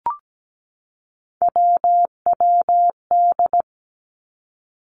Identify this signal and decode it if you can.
Morse code, heard as a keyed tone, about 17 wpm, 700 Hz. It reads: WWD
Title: Three Challenging Letter-Number Combinations 17wpm